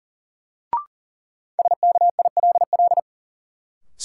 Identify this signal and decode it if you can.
Morse code, heard as a keyed tone, about 40 wpm, 700 Hz. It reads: SKILL
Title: Sentences from Top 1200 Words - New Word Drill Alt - 40wpm